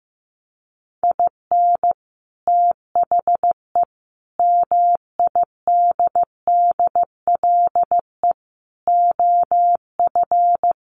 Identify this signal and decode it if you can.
Morse code, heard as a keyed tone, about 15 wpm, 700 Hz. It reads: IN THE MIDDLE OF
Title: Sets of 5 Words 15wpm